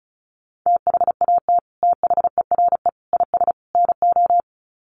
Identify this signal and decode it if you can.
Morse code, heard as a keyed tone, about 34 wpm, 700 Hz. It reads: THAT THERE IS NO
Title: Sets of 4 Words 35wpm